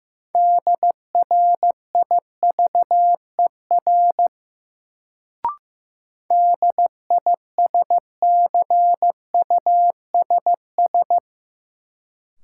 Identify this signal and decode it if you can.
Morse code, heard as a keyed tone, about 15 wpm, 700 Hz. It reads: DRIVER DISCUSS
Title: Sentences from Top 1000 Words - New Word Drill - 15wpm